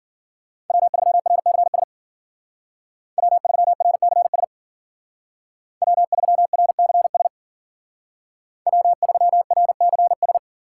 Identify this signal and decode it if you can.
Morse code, heard as a keyed tone, about 52 wpm, 700 Hz. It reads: W3RCS W3RCS W3RCS W3RCS